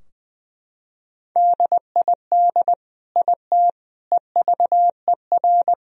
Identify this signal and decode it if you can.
Morse code, heard as a keyed tone, about 20 wpm, 700 Hz. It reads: DID IT EVER
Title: Sentences from Top 1000 Words - Review Part 2 - 20wpm